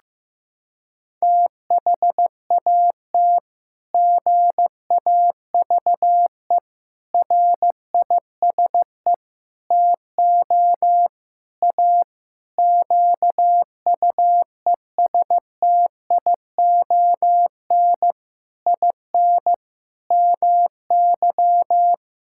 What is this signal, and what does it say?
Morse code, heard as a keyed tone, about 15 wpm, 700 Hz. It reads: THAT GAVE RISE TO A QUESTION IN MY
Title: Sentences from Top 1000 Words - Review Part 6 - 15wpm